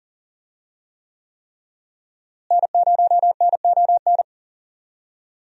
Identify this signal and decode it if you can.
Morse code, heard as a keyed tone, about 40 wpm, 700 Hz. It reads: N0NON
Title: USA Call Signs - 5 Characters - Repeated 3x - 40wpm